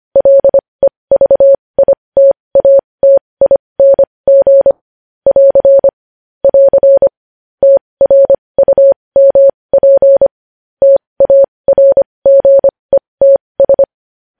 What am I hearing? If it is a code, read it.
Morse code, heard as a keyed tone, about 25 wpm, spaced as for 20 wpm, 550 Hz. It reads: LEVITATING <AR> <AR> TRUMP TARGETS
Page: news-headlines-in-morse-code-at-20-wpm